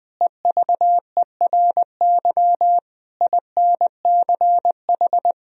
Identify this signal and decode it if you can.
Morse code, heard as a keyed tone, about 20 wpm, 700 Hz. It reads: EVERY INCH
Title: Sentences from Top 1000 Words - Review Part 3 - 20wpm